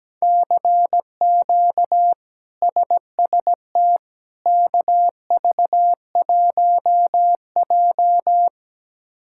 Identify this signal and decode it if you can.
Morse code, heard as a keyed tone, about 17 wpm, 700 Hz. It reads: CQ SST KV1J
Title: K1USN SST QSO Practice - 17wpm